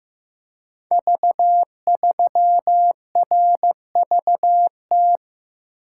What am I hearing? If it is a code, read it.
Morse code, heard as a keyed tone, about 15 wpm, 700 Hz. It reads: V3RVT
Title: Code Groups - 5 Letter:Numbers - 15wpm